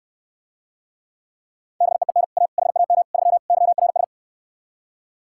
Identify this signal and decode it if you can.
Morse code, heard as a keyed tone, about 68 wpm, 700 Hz. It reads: BEEN A HAM 3 YRS